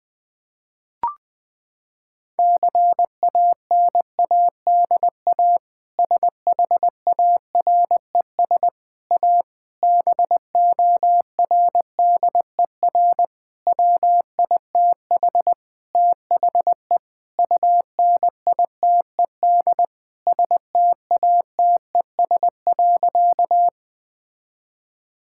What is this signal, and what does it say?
Morse code, heard as a keyed tone, about 20 wpm, 700 Hz. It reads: CANADA SHARES A BORDER WITH THE UNITED STATES.
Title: Sentences with 4-Word Phrases - 20wpm